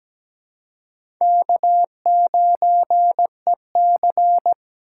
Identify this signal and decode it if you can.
Morse code, heard as a keyed tone, about 17 wpm, 700 Hz. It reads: K9EC
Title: USA Call Signs - 4 Characters - 2x Character Spacing - 17wpm